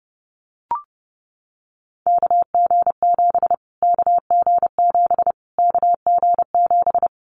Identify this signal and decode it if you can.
Morse code, heard as a keyed tone, about 30 wpm, 700 Hz. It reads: KG7 KG7 KG7
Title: Callsign Prefixes for Top 25 DXCC - Repeated 3x - 30wpm